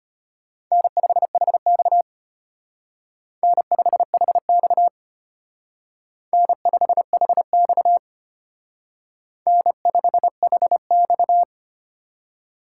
Morse code, heard as a keyed tone, about 32 wpm, 700 Hz. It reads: N5HX N5HX N5HX N5HX